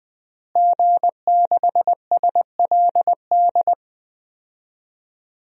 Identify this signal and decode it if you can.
Morse code, heard as a keyed tone, about 20 wpm, 700 Hz. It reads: G6SLD
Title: DXCC-100 Call Signs - 5 Characters - Repeated 3x - 20wpm